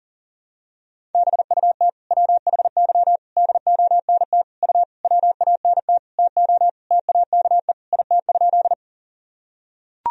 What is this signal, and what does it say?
Morse code, heard as a keyed tone, about 40 wpm, 700 Hz. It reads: BUT WHY DONT U WANT TO TAKE IT?
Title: Sentences from Top 300 Words and Abbreviations 40wpm